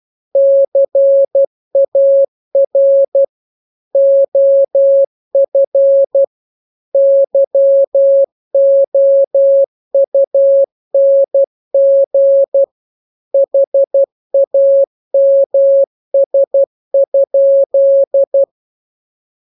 Morse code, heard as a keyed tone, about 12 wpm, 550 Hz. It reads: CAR OF YOUNG HAMS?